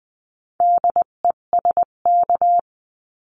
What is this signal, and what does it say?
Morse code, heard as a keyed tone, about 20 wpm, 700 Hz. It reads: DESK
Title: 258 Lesson 42c - Words 16-20wpm